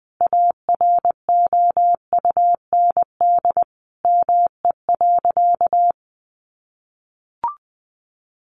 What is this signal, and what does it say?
Morse code, heard as a keyed tone, about 20 wpm, 700 Hz. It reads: AROUND ME.
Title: Sentences from Top 1500 Words 20wpm